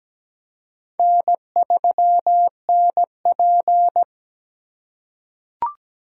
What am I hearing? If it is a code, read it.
Morse code, heard as a keyed tone, about 17 wpm, 700 Hz. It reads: N3NP
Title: USA Call Signs - 4 Characters - 3x Character Spacing - 17wpm